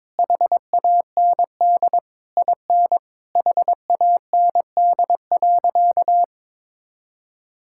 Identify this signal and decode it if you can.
Morse code, heard as a keyed tone, about 22 wpm, 700 Hz. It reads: HAND IN HAND.
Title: Sentences from Top 1000 Words - Review Part 5 - 22wpm